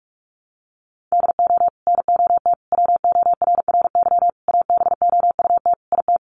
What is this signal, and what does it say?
Morse code, heard as a keyed tone, about 44 wpm, 700 Hz. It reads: DO NOT WORRY ABOUT IT
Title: Sets of 5 Words 45wpm